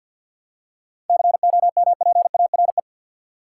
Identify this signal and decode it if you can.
Morse code, heard as a keyed tone, about 52 wpm, 700 Hz. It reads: COMPARE
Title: Seven Letter Words - 50wpm